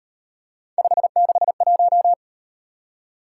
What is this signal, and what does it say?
Morse code, heard as a keyed tone, about 38 wpm, 700 Hz. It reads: 561